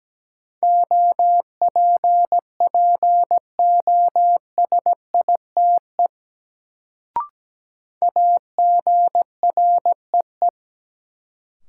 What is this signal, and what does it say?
Morse code, heard as a keyed tone, about 17 wpm, 700 Hz. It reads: OPPOSITE AGREE
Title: Sets of 5 Words Encore - New Word Drill - 17wpm